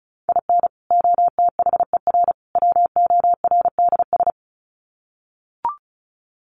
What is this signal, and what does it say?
Morse code, heard as a keyed tone, about 34 wpm, 700 Hz. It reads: IN OTHER WORDS
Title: Sets of 3 Words 35wpm